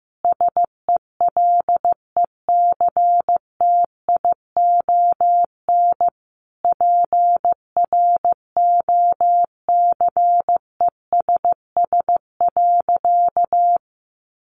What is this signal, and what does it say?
Morse code, heard as a keyed tone, about 15 wpm, 700 Hz. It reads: SELECTION PROCESS.
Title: Sentences from Top 1400 Words 15wpm